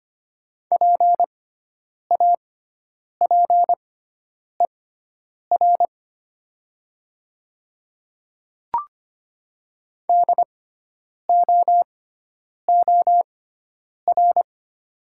Morse code, heard as a keyed tone, about 25 wpm, spaced as for 9 wpm, 700 Hz. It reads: PAPER DOOR
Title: 6x Character Spacing - Top 300 Words - 25wpm